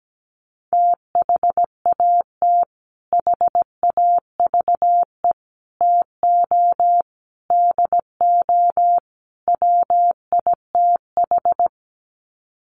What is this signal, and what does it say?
Morse code, heard as a keyed tone, about 17 wpm, 700 Hz. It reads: THAT HAVE TO DO WITH
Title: Sets of 5 Words Encore2 17wpm